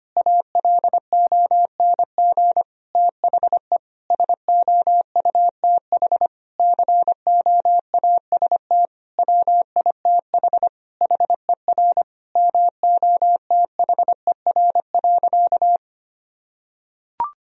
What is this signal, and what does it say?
Morse code, heard as a keyed tone, about 25 wpm, 700 Hz. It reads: ALONG THE SOUTH COAST WITH HER MOTHER.
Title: Sentences from Top 1000 Words - Review Part 4 - 25wpm